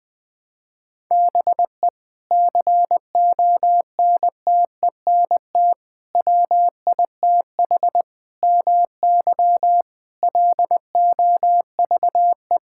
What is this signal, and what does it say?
Morse code, heard as a keyed tone, about 20 wpm, 700 Hz. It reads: BE CONTENT WITH MY LOVE
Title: Sentences from Top 1000 Words - Review Part 6 - 20wpm